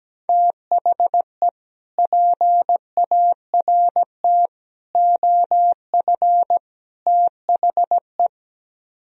Morse code, heard as a keyed tone, about 17 wpm, 700 Hz. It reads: THE PART OF THE
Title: Speed-Racing - Sets of 4 Words 17wpm